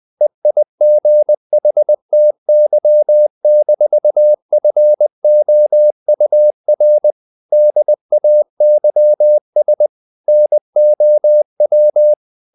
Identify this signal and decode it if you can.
Morse code, heard as a keyed tone, about 20 wpm, 600 Hz. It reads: EIGHTY-FOUR DAYS NOW
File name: the old man and the sea / the old man and the sea part 1 20wpm